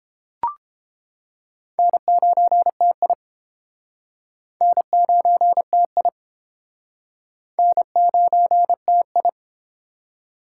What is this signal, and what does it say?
Morse code, heard as a keyed tone, about 31 wpm, 700 Hz. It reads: N9TI N9TI N9TI